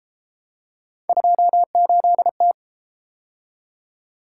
Morse code, heard as a keyed tone, about 31 wpm, 700 Hz. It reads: 28T